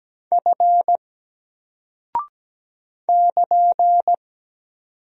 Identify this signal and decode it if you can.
Morse code, heard as a keyed tone, about 17 wpm, 700 Hz. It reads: F <KN>
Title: Single Character (Letter:Number:Punctuation:Prosign) Alt 17wpm